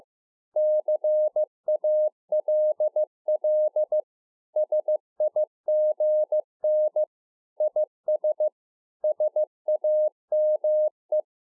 Morse code, heard as a keyed tone, about 15 wpm, 610 Hz. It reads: CALL SIGN IS SAME